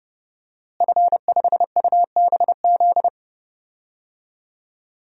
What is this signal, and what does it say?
Morse code, heard as a keyed tone, about 31 wpm, 700 Hz. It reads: F5UBZ
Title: DXCC-100 Call Signs - 5 Characters - Repeated 3x - 30wpm